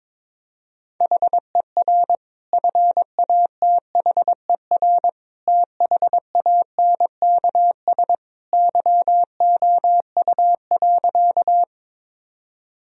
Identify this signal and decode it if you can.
Morse code, heard as a keyed tone, about 22 wpm, 700 Hz. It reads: HER FATHER THANKS YOU.
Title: Sentences from Top 1000 Words - Review Part 3 - 22wpm